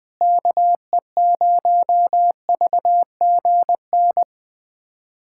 Morse code, heard as a keyed tone, about 20 wpm, 700 Hz. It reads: KE0VGN
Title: USA Callsign DE Callsign 20wpm